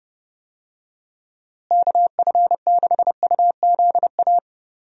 Morse code, heard as a keyed tone, about 30 wpm, 700 Hz. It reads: KF6UZA
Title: USA Call Signs - 6 Characters - Repeated 3x - 30wpm